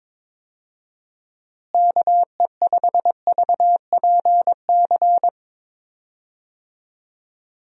Morse code, heard as a keyed tone, about 22 wpm, 700 Hz. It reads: KE5VPC
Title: USA Call Signs - 6 Characters - Repeated 3x - 22wpm